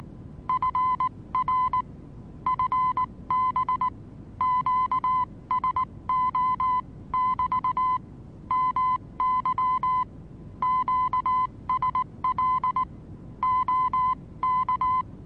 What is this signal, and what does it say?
Morse code, heard as a keyed tone, about 19 wpm, spaced as for 16 wpm, 1020 Hz. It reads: FR FB QSO<BT> MY QSL OK
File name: repos / Morsekurs / DJ4UF Morsen26